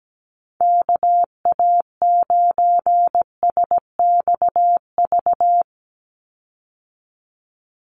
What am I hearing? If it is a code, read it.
Morse code, heard as a keyed tone, about 17 wpm, 700 Hz. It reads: KA9SXV
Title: Call Signs Repeated 3x - 17wpm